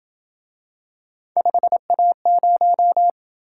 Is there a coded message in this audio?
Yes: Morse code from 1.4 to 3.1 s, heard as a keyed tone, about 27 wpm, 700 Hz. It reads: 5A0